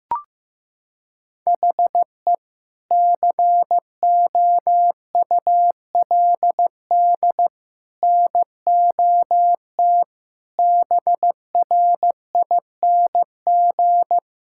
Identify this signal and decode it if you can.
Morse code, heard as a keyed tone, about 15 wpm, 700 Hz. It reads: HE COULD NOT BRING
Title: Sentences from Top 1600 Words 15wpm